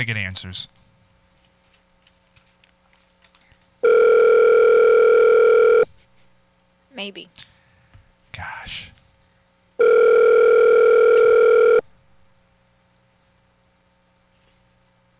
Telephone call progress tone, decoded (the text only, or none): ringback tone